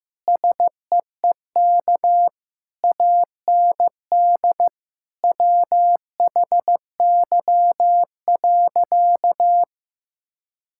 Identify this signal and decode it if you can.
Morse code, heard as a keyed tone, about 15 wpm, 700 Hz. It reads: SEEK AND WHY.
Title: Sentences from Top 1600 Words 15wpm